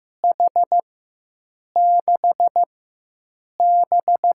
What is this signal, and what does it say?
Morse code, heard as a keyed tone, about 15 wpm, 700 Hz. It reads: H 6 B
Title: USA Call Signs - 4 Characters - 4x Character Spacing - 15wpm